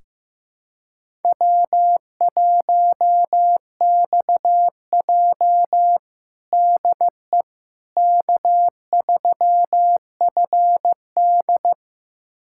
Morse code, heard as a keyed tone, about 16 wpm, 700 Hz. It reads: W1XJ DE K3FD
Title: USA Callsign DE Callsign - 4 Characters - 15wpm